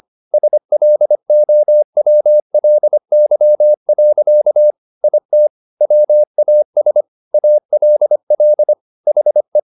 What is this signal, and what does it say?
Morse code, heard as a keyed tone, about 25 wpm, 600 Hz. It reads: SLOWLY. IT WAS ALL HE
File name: the old man and the sea part 1 25wpm